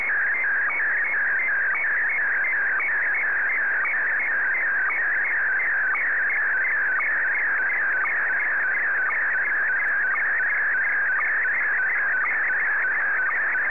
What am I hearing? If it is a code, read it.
SSTV (slow-scan television), Scottie DX.